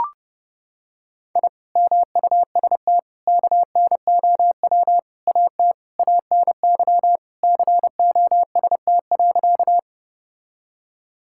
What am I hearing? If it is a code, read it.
Morse code, heard as a keyed tone, about 30 wpm, 700 Hz. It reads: I MUST KNOW AT ANY COST.
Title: Sentences from Top 1000 Words - Review Part 5 - 30wpm